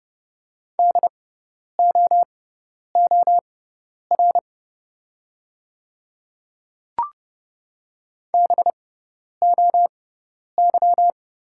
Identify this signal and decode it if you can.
Morse code, heard as a keyed tone, about 32 wpm, spaced as for 10 wpm, 700 Hz. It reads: DOOR BOY
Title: 6x Character Spacing - Top 500 Words - 30wpm